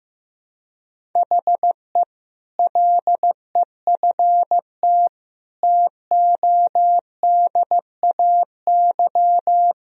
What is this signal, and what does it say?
Morse code, heard as a keyed tone, about 15 wpm, 700 Hz. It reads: HE LEFT TODAY
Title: Sentences from Top 1000 Words - Review Part 2 - 15wpm